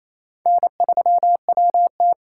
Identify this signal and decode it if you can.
Morse code, heard as a keyed tone, about 33 wpm, 700 Hz. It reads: N3WT
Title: Speed Racing - USA 1x2 Call Signs 28wpm